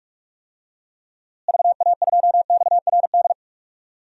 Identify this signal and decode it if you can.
Morse code, heard as a keyed tone, about 46 wpm, 700 Hz. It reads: VA2XRD
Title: DXCC-100 Call Signs - 6 Characters - Repeated 3x - 45wpm